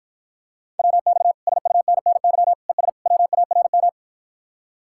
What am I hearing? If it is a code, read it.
Morse code, heard as a keyed tone, about 52 wpm, 700 Hz. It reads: WX SUNNY ES WARM